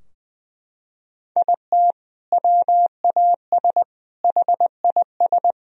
Morse code, heard as a keyed tone, about 20 wpm, 700 Hz. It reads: IT WAS HIS
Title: Sentences from Top 1000 Words - Review Part 3 - 20wpm